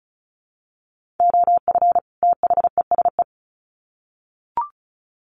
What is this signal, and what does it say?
Morse code, heard as a keyed tone, about 34 wpm, 700 Hz. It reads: OF THESE
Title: Sets of 2 Words 35wpm